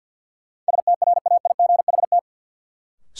Morse code, heard as a keyed tone, about 52 wpm, 700 Hz. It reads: STRAIGHT
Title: Sentences from Top 1000 Words - New Word Drill - 50wpm